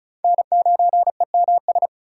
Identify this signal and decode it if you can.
Morse code, heard as a keyed tone, about 34 wpm, 690 Hz. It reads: N9EMS
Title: USA Call Signs - 4x Character Spacing - 35wpm